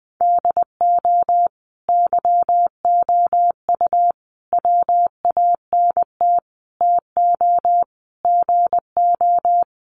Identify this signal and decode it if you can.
Morse code, heard as a keyed tone, about 20 wpm, 700 Hz. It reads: DO YOU WANT TO GO
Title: Sets of 5 Words Encore2 20wpm